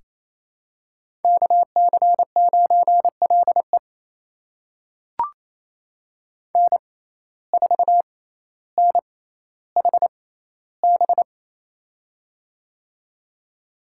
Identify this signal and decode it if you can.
Morse code, heard as a keyed tone, about 28 wpm, spaced as for 14 wpm, 700 Hz. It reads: KC9LE N4NHB
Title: USA Call Signs - 6x Character Spacing - 28wpm